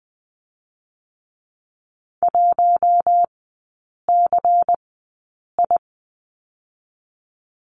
Morse code, heard as a keyed tone, about 20 wpm, spaced as for 8 wpm, 700 Hz. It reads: 1CI
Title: 138 Lesson 23e - Letter f - Three Characters 20wpm